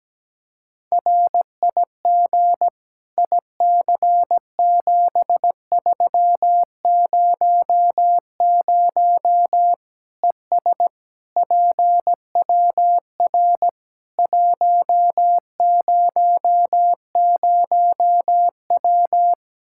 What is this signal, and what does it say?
Morse code, heard as a keyed tone, about 17 wpm, 700 Hz. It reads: RIG IC7300 ES PWR 100W
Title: Practice QSO - Exchange 2 - Info - 17wpm